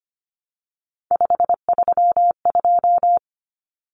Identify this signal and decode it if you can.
Morse code, heard as a keyed tone, about 25 wpm, 700 Hz. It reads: 532
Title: Common Cut Numbers - 3-Digit 25wpm